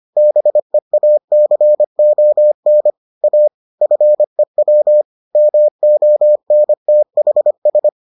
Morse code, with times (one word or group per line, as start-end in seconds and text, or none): 0.2-2.9 BEACON
3.2-3.5 A
3.8-5.0 FEW
5.4-7.9 MONTHS